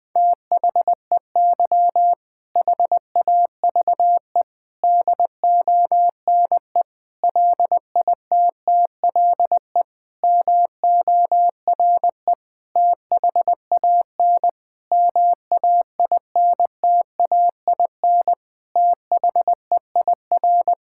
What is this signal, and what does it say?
Morse code, heard as a keyed tone, about 19 wpm, 700 Hz. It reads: THEY HAVE DONE LITTLE MORE THAN MAINTAIN THEIR
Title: Sentences from Top 1000 Words - Review Part 5 - 20wpm